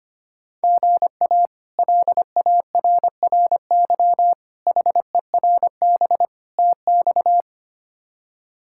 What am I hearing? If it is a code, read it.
Morse code, heard as a keyed tone, about 25 wpm, 700 Hz. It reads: GA LARRY HERB TX
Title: K1USN SST QSO Practice - 25wpm